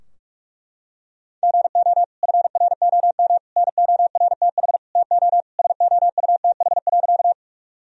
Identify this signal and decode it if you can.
Morse code, heard as a keyed tone, about 46 wpm, 700 Hz. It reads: GO FROM NORTH TO SOUTH.
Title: Sentences from Top 1000 Words - Review Part 2 - 45wpm